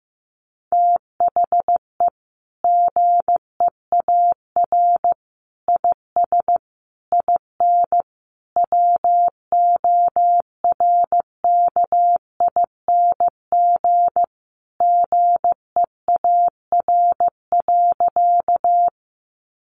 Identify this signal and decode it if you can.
Morse code, heard as a keyed tone, about 15 wpm, 700 Hz. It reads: THE GEAR IS IN WORKING GEAR.
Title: Sentences from Top 1400 Words 15wpm